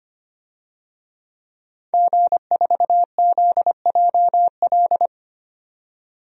Morse code, heard as a keyed tone, about 25 wpm, 700 Hz. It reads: G4ZJL
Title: DXCC-100 Call Signs - 5 Characters - 25wpm